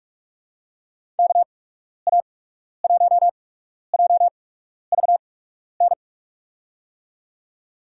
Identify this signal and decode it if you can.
Morse code, heard as a keyed tone, about 43 wpm, spaced as for 12 wpm, 700 Hz. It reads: KA1JVN
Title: USA Call Signs - 8x Character Spacing - 45wpm